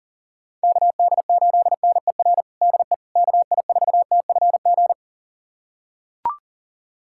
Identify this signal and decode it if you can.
Morse code, heard as a keyed tone, about 37 wpm, 700 Hz. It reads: KD8NER DE KI4TFG
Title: USA Callsign DE Callsign - 6 Characters - 40wpm